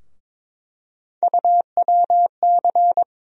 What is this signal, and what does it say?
Morse code, heard as a keyed tone, about 22 wpm, 700 Hz. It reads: UWC
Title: USA Call Sign Postfixes - 3 Letters 22wpm